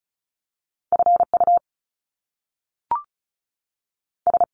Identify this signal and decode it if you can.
Morse code, heard as a keyed tone, about 36 wpm, spaced as for 6 wpm, 700 Hz. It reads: FUS
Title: Top Two Letter Combinations in Words 35wpm